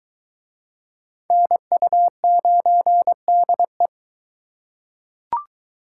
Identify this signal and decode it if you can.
Morse code, heard as a keyed tone, about 22 wpm, 700 Hz. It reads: NU9DE